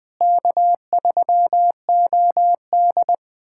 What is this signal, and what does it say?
Morse code, heard as a keyed tone, about 20 wpm, 700 Hz. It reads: K3OD
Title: USA Callsign DE Callsign - 4 Characters - 20wpm